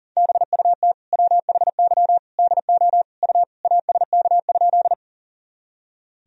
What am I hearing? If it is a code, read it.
Morse code, heard as a keyed tone, about 40 wpm, 700 Hz. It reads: BUT WHY DO U ASK?
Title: Sentences from Top 200 Words and Abbreviations 40wpm